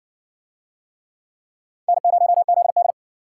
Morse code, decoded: N0ZD

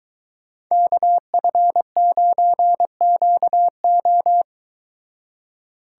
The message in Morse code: KF9QO

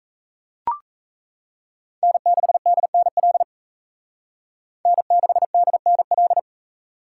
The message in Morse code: N6DNL N6DNL

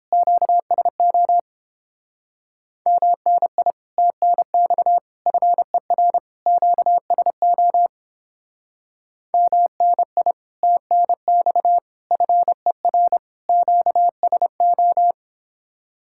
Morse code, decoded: QSO MNI TNX FER QSO MNI TNX FER QSO

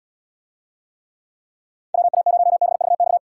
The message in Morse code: WA0BVX

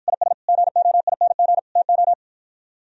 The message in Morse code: IS GOING TO

